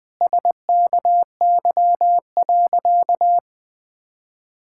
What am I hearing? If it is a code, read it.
Morse code, heard as a keyed tone, about 20 wpm, 700 Hz. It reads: SKY.
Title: Sentences with 9-Word Phrases - 20wpm